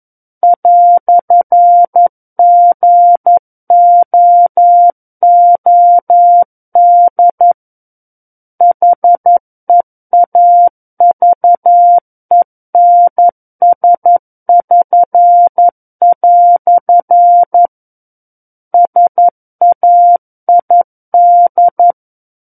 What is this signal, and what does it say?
Morse code, heard as a keyed tone, about 11 wpm, 700 Hz. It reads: "GOOD HEAVENS<SN>" SAID
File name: 11 WPM